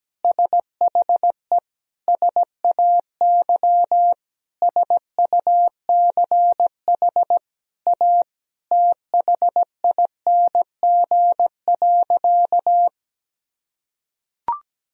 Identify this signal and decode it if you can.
Morse code, heard as a keyed tone, about 17 wpm, 700 Hz. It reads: SHE SAY SUCH A THING.
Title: Sentences from Top 200 Words and Abbreviations 17wpm